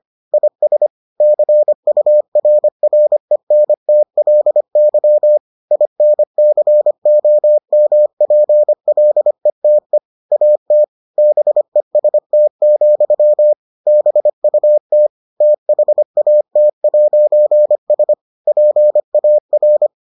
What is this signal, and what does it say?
Morse code, heard as a keyed tone, about 25 wpm, 600 Hz. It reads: IS CURRENTLY INCOMPLETE AT BEST, BUT THAT'S PAR